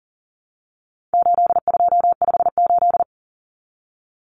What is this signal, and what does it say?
Morse code, heard as a keyed tone, about 40 wpm, 700 Hz. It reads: 8258
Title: Common Cut Numbers - 4-Digit 40wpm